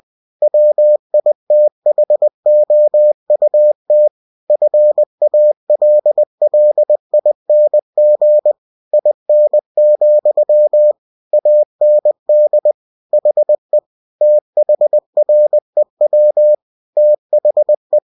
Morse code, heard as a keyed tone, about 20 wpm, 600 Hz. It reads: WITHOUT FALLING IN, AND HE THREW THE
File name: cw-books / winnie-the-pooh / winnie-20wpm-page-0092